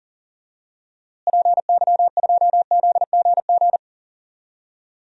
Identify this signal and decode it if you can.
Morse code, heard as a keyed tone, about 38 wpm, 700 Hz. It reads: PY2ZGG
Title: DXCC-100 Call Signs - 6 Characters - Repeated 2x - 40wpm